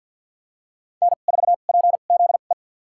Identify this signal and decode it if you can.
Morse code, heard as a keyed tone, about 48 wpm, spaced as for 35 wpm, 700 Hz. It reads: N4PZE
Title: USA Call Signs - 5 Characters - 2x Character Spacing - 50wpm